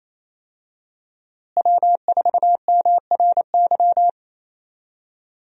Morse code, heard as a keyed tone, about 28 wpm, 700 Hz. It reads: W4MRY
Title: USA Call Signs - 5 Characters - Repeated 2x - 28wpm